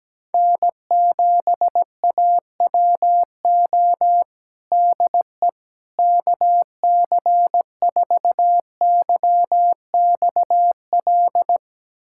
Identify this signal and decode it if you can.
Morse code, heard as a keyed tone, about 17 wpm, 700 Hz. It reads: N7AWO DE KC4YXL
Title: USA Callsign DE Callsign 17wpm